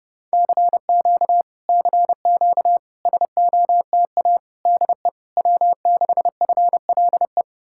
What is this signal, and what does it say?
Morse code, heard as a keyed tone, about 33 wpm, 700 Hz. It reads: CQ CQ SOTA DE W6FLE